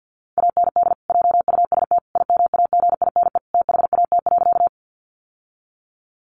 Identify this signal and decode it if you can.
Morse code, heard as a keyed tone, about 50 wpm, 700 Hz. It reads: AND JUST IMAGINE THAT.
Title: Sentences from Top 1400 Words 50wpm